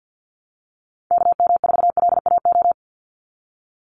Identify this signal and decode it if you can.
Morse code, heard as a keyed tone, about 50 wpm, 700 Hz. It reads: KM4LAO
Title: Call Signs Repeated 3x - 50wpm